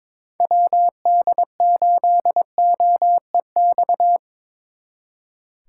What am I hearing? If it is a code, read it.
Morse code, heard as a keyed tone, about 27 wpm, 700 Hz. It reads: WD8OEX
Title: Speed Racing - USA Call Signs 22wpm